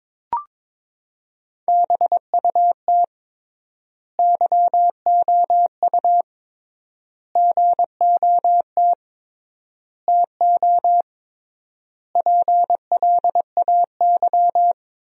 Morse code, heard as a keyed tone, about 22 wpm, 700 Hz. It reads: BUT YOU GOT TO PLAY
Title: 3x Word-Spacing - Sentences from Top 300 Words - 22wpm